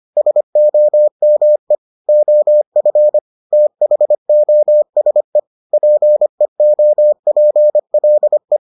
Morse code, heard as a keyed tone, about 25 wpm, 600 Hz. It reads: SOME OF THOSE PEOPLE